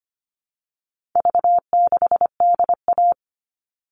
Morse code, heard as a keyed tone, about 25 wpm, 700 Hz. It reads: V6DA
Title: All Cut Numbers - 4-Digit 25wpm